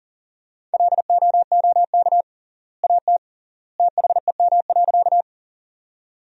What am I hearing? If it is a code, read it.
Morse code, heard as a keyed tone, about 38 wpm, 700 Hz. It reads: LOOK AT THEM.
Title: 3x Word-Spacing - Sentences from Top 100 Words - 40wpm